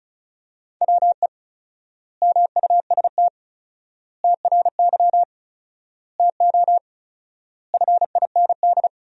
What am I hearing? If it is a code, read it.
Morse code, heard as a keyed tone, about 36 wpm, 700 Hz. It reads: WE MUST TRY TO FIND
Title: 4x Word-Spacing - Sentences from Top 400 Words - 35wpm